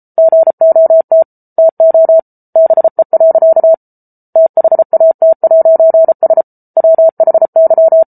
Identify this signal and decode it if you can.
Morse code, heard as a keyed tone, about 34 wpm, 650 Hz. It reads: GOT TO BE. THAT'S WHY